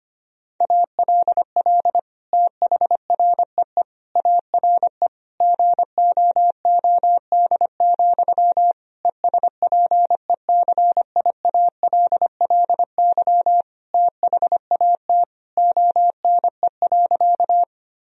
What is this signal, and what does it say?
Morse code, heard as a keyed tone, about 25 wpm, 700 Hz. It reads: ALL THREE ARE GOOD, ESPECIALLY THAT ONE.
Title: Sentences from Top 1000 Words - Review Part 4 - 25wpm